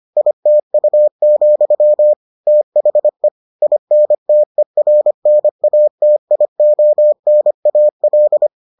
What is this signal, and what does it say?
Morse code, heard as a keyed tone, about 25 wpm, 600 Hz. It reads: ITU, THE INTERNATIONAL